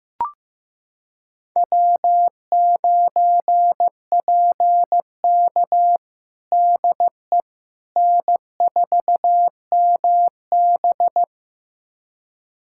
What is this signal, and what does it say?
Morse code, heard as a keyed tone, about 15 wpm, 700 Hz. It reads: W9PK DE N4MB
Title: USA Callsign DE Callsign - 4 Characters - 15wpm